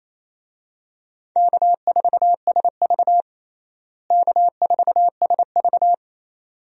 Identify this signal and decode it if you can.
Morse code, heard as a keyed tone, about 28 wpm, 700 Hz. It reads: K4SV K4SV
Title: USA Call Signs - 4 Characters - Repeated 2x - 28wpm